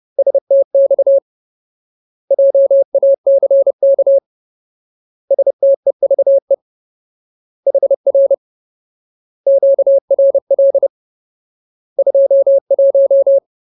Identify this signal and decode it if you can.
Morse code, heard as a keyed tone, about 30 wpm, 550 Hz. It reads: STX JACK STEVE HR QRL 21
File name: Ultimate Word List / 30WPM Ultimate